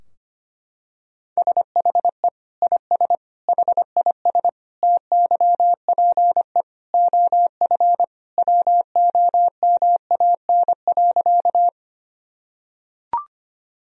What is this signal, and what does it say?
Morse code, heard as a keyed tone, about 25 wpm, 700 Hz. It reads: SHE IS HIS TYPE OF WOMAN.
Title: Sentences from Top 1000 Words - Review Part 3 - 25wpm